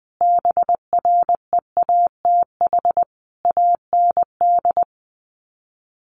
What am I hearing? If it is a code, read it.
Morse code, heard as a keyed tone, about 20 wpm, 700 Hz. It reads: BREATH AND
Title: Sets of 5 Words Encore 20wpm